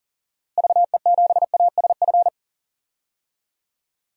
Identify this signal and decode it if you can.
Morse code, heard as a keyed tone, about 40 wpm, 700 Hz. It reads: VE7ASF
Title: DXCC-100 Call Signs - 6 Characters - Repeated 3x - 40wpm